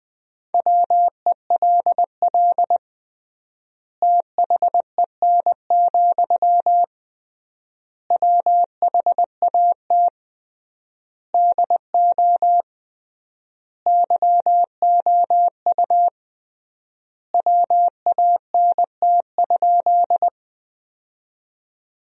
Morse code, heard as a keyed tone, about 20 wpm, 700 Hz. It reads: WELL THEN, WHAT DO YOU WANT?
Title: 3x Word-Spacing - Sentences from Top 200 Words - 20wpm